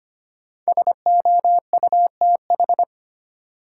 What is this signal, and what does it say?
Morse code, heard as a keyed tone, about 25 wpm, 700 Hz. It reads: SOUTH
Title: Sentences from Top 400 Words - New Word Drill - 25wpm